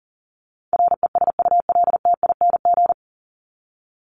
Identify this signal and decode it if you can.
Morse code, heard as a keyed tone, about 40 wpm, 700 Hz. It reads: RESULTING
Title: ING Suffix - Top 100 Words with Base Words 40wpm